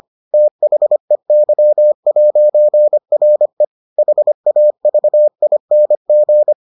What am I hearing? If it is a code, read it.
Morse code, heard as a keyed tone, about 25 wpm, 600 Hz. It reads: THEY'RE HAVING